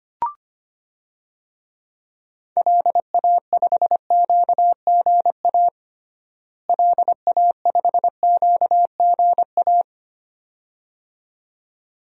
Morse code, heard as a keyed tone, about 25 wpm, 700 Hz. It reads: LA5QGA LA5QGA
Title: DXCC-100 Call Signs - 6 Characters - Repeated 2x - 25wpm